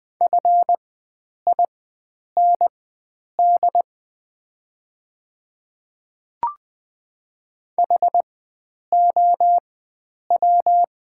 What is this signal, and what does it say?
Morse code, heard as a keyed tone, about 20 wpm, spaced as for 9 wpm, 700 Hz. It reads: FIND HOW
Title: 4x Character Spacing - Top 100 Words - 20wpm